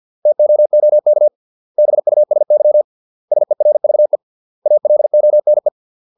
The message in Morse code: TOOK BUSY SERVE ALONE

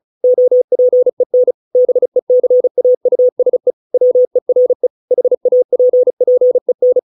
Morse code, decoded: OPEN BECAUSE WERE HAPPEN

